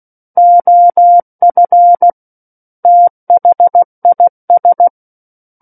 Morse code, heard as a keyed tone, about 16 wpm, 700 Hz. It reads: OF THIS
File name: Book One / 07-Chapter 7 - 16 WPM